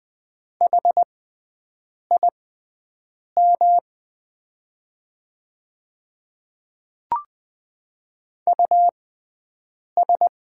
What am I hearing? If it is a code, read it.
Morse code, heard as a keyed tone, about 20 wpm, spaced as for 7 wpm, 700 Hz. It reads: HIM US
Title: 6x Character Spacing - Top 500 Words - 20wpm